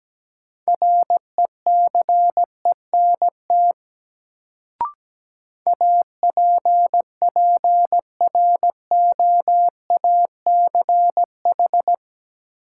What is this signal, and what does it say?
Morse code, heard as a keyed tone, about 17 wpm, 700 Hz. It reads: RECENT APPROACH
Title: Sentences from Top 1400 Words - New Word Drill Alt - 17wpm